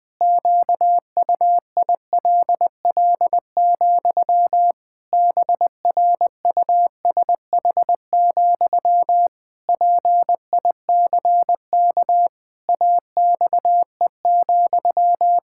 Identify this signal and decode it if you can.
Morse code, heard as a keyed tone, about 20 wpm, 700 Hz. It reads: QUILL, BRUSH, PICK AXE,
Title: Pangram Challenge v2 - 20wpm